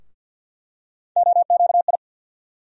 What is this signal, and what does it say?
Morse code, heard as a keyed tone, about 51 wpm, 700 Hz. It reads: OQI
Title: USA Call Sign Postfixes - 3 Letters 50wpm